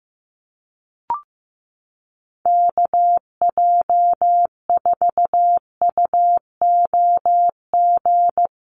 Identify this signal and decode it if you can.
Morse code, heard as a keyed tone, about 15 wpm, 700 Hz. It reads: KJ4UOG
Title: USA Call Signs Repeated 3x 15wpm